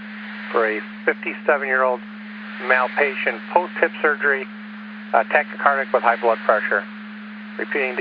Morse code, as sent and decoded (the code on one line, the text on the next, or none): none